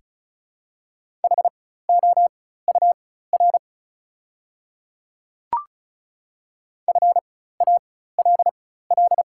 Morse code, heard as a keyed tone, about 36 wpm, spaced as for 16 wpm, 690 Hz. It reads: HOUR FALL
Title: 4x Character Spacing - Top 500 Words - 35wpm